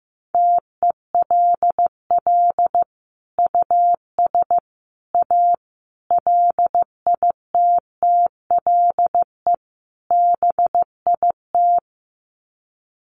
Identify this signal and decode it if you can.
Morse code, heard as a keyed tone, about 15 wpm, 700 Hz. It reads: TELL US A LITTLE BIT
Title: Sets of 5 Words Encore2 15wpm